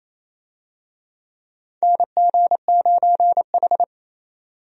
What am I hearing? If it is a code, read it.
Morse code, heard as a keyed tone, about 28 wpm, 700 Hz. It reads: NG9H
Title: USA Call Signs - 4 Characters - 28wpm